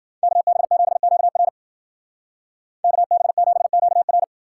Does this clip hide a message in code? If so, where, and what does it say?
Yes: Morse code from 0.2 to 4.2 s, heard as a keyed tone, about 60 wpm, 700 Hz. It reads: KB7QR KB7QR